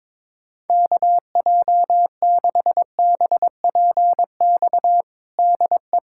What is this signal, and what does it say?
Morse code, heard as a keyed tone, about 22 wpm, 700 Hz. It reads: KJ6BPX DE
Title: USA Callsign DE Callsign - 6 Characters - 22wpm